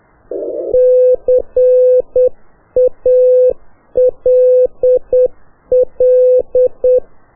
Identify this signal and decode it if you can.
Morse code, heard as a keyed tone, about 8 wpm, 2600 Hz, with noise. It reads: CALL
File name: qso1-qrs